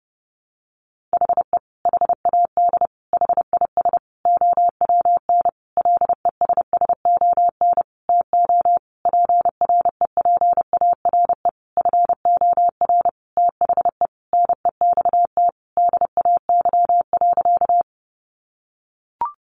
Morse code, heard as a keyed tone, about 30 wpm, 700 Hz. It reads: HE HAD HIS OWN LESSON TO PREPARE FOR THE NEXT DAY.
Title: Sentences from Top 1200 Words 30wpm